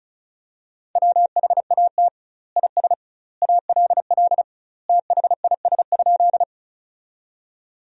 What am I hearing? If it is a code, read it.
Morse code, heard as a keyed tone, about 36 wpm, 700 Hz. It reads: WHAT IS ALL THIS?
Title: 2x Word-Spacing - Sentences from Top 100 Words - 35wpm